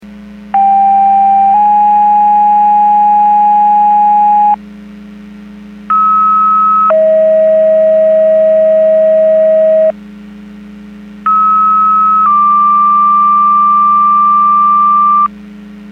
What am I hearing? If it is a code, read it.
Two-tone page, heard as a one-second tone, then a three-second tone, about 790 then 830 Hz, about 1290 then 640 Hz, about 1290 then 1180 Hz, with noise. It is heard from 0.5 to 15.3 s.
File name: Portage / 00732 21-26 Portage 154.385 Paging